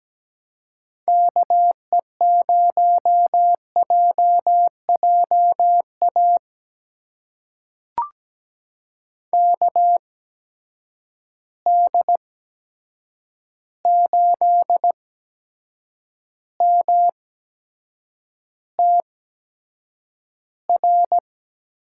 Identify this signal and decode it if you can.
Morse code, heard as a keyed tone, about 17 wpm, spaced as for 7 wpm, 700 Hz. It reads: KE0JJA KD8MTR
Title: USA Call Signs - 8x Character Spacing - 17wpm